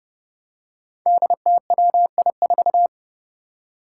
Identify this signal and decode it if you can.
Morse code, heard as a keyed tone, about 30 wpm, 700 Hz. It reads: DTWI4
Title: Five Challenging Letter-Number Combinations 30wpm